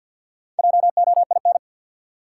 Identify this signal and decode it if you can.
Morse code, heard as a keyed tone, about 52 wpm, 700 Hz. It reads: JOIN
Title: Sentences from Top 1400 Words - New Word Drill Alt - 50wpm